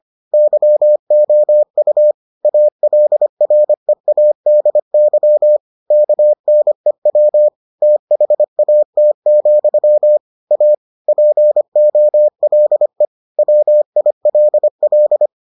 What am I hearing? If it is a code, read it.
Morse code, heard as a keyed tone, about 25 wpm, 600 Hz. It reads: YOU ALREADY KNEW THAT, A POLE WILL